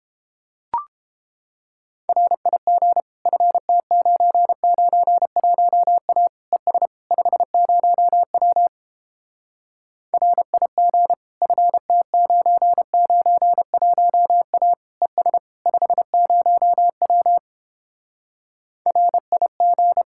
Morse code, heard as a keyed tone, about 31 wpm, 700 Hz. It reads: RIG FT991A ES 50W RIG FT991A ES 50W RIG